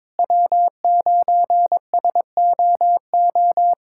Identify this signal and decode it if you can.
Morse code, heard as a keyed tone, about 22 wpm, 700 Hz. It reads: W9SOO
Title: USA Callsign DE Callsign - 5 Characters - 22wpm